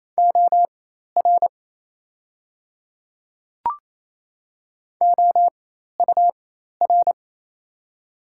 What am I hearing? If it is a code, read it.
Morse code, heard as a keyed tone, about 27 wpm, spaced as for 13 wpm, 690 Hz. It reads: OR OUR
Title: 4x Character Spacing - Top 200 Words - 28wpm